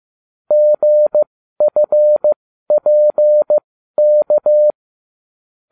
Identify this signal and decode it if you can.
Morse code, heard as a keyed tone, about 15 wpm, spaced as for 12 wpm, 600 Hz. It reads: GFPK